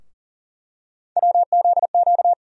Morse code, heard as a keyed tone, about 40 wpm, 700 Hz. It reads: WZQ
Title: USA Call Sign Postfixes - 3 Letters 40wpm